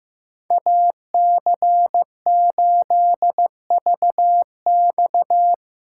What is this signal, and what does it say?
Morse code, heard as a keyed tone, about 15 wpm, 700 Hz. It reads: AC8VX
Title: USA Call Signs - 5 Characters - 2x Character Spacing - 15wpm